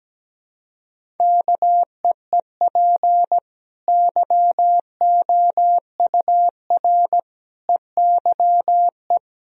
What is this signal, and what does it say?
Morse code, heard as a keyed tone, about 17 wpm, 700 Hz. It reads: KEEP YOUR EYE
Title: Sentences from Top 1000 Words - Review Part 1 - 17wpm